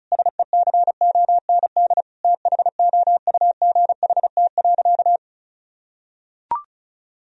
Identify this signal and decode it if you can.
Morse code, heard as a keyed tone, about 34 wpm, 700 Hz. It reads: SECOND THOUGHT.
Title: Sentences from Top 1000 Words - Review Part 1 - 35wpm